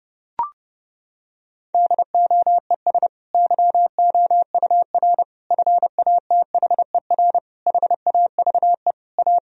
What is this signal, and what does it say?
Morse code, heard as a keyed tone, about 30 wpm, 700 Hz. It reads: DOES YOUR FATHER HAVE A
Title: Sentences from Top 1000 Words - Review Part 3 - 30wpm